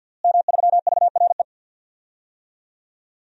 Morse code, heard as a keyed tone, about 50 wpm, 700 Hz. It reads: M3VRE
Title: DXCC-100 Call Signs - 5 Characters - Repeated 3x - 50wpm